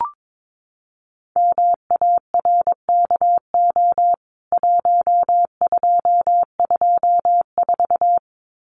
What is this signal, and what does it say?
Morse code, heard as a keyed tone, about 22 wpm, 700 Hz. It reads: MARKO 1224
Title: CWT Exchange 22wpm